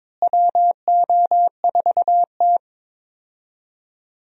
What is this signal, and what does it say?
Morse code, heard as a keyed tone, about 22 wpm, 700 Hz. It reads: WO4T
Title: USA Call Signs - 4 Characters - Repeated 3x - 22wpm